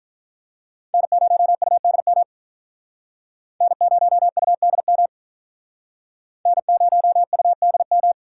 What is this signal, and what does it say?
Morse code, heard as a keyed tone, about 47 wpm, 700 Hz. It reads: N0UDM N0UDM N0UDM